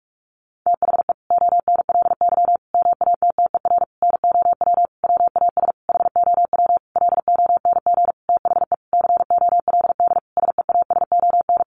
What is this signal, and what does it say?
Morse code, heard as a keyed tone, about 44 wpm, 700 Hz. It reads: THE ONLY MATTER NOW WAS HOW LONG THE COLD SEASON
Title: Sentences from Top 1400 Words 45wpm